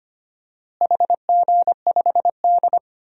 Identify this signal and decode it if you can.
Morse code, heard as a keyed tone, about 25 wpm, 700 Hz. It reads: HG5D
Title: DXCC-100 Call Signs - 4 Characters - Repeated 3x - 25wpm